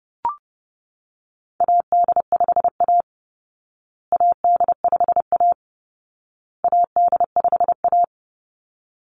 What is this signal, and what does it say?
Morse code, heard as a keyed tone, about 30 wpm, 700 Hz. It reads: AD5A AD5A AD5A
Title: Call Signs Repeated 2x - 30wpm